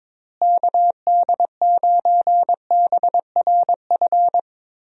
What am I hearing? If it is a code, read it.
Morse code, heard as a keyed tone, about 22 wpm, 700 Hz. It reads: KD9BRF
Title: USA Call Signs - 6 Characters - Repeated 2x - 22wpm